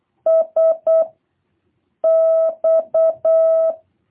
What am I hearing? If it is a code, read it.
Morse code, heard as a keyed tone, about 8 wpm, 650 Hz. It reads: S X